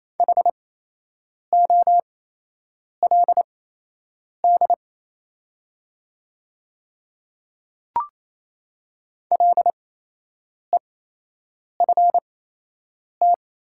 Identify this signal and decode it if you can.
Morse code, heard as a keyed tone, about 27 wpm, spaced as for 8 wpm, 700 Hz. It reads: HOLD LEFT
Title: 8x Character Spacing - Top 500 Words - 28wpm